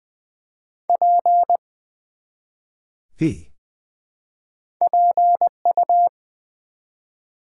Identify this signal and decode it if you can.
Morse code, heard as a keyed tone, about 20 wpm, 700 Hz. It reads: P PU
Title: Build A Word - 1 Letter At A Time - 3-Letter Words - 20wpm